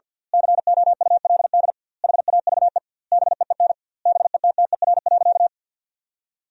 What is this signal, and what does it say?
Morse code, heard as a keyed tone, about 50 wpm, 700 Hz. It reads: COULD HAVE BEEN BETTER.
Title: 2x Word-Spacing - Sentences from Top 500 Words - 50wpm